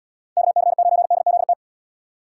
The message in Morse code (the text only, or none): KC9DZT